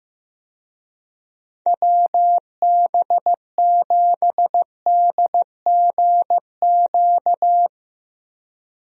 Morse code, heard as a keyed tone, about 16 wpm, 700 Hz. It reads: WB7DGQ
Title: USA Call Signs - 6 Characters - Repeated 2x - 15wpm